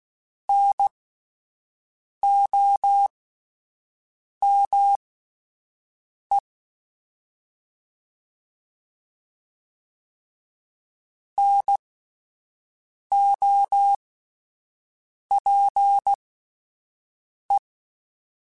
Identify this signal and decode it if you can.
Morse code, heard as a keyed tone, about 16 wpm, spaced as for 5 wpm, 800 Hz. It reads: NOME NOPE